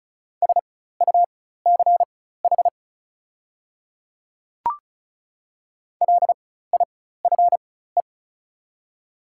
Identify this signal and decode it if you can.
Morse code, heard as a keyed tone, about 34 wpm, spaced as for 16 wpm, 690 Hz. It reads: SUCH LIFE
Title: 4x Character Spacing - Top 300 Words - 35wpm